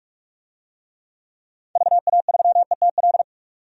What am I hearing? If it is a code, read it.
Morse code, heard as a keyed tone, about 46 wpm, 700 Hz. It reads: VA3ETL
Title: DXCC-100 Call Signs - 6 Characters - 45wpm